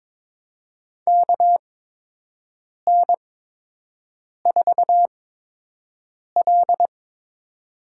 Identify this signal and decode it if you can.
Morse code, heard as a keyed tone, about 21 wpm, spaced as for 6 wpm, 700 Hz. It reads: KN4L
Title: USA Call Signs - 4 Characters - 8x Character Spacing - 22wpm